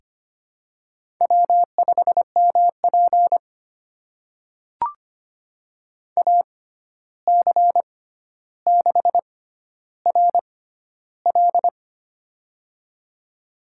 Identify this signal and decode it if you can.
Morse code, heard as a keyed tone, about 25 wpm, spaced as for 12 wpm, 700 Hz. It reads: W5MP AC6RL
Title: USA Call Signs - 6x Character Spacing - 25wpm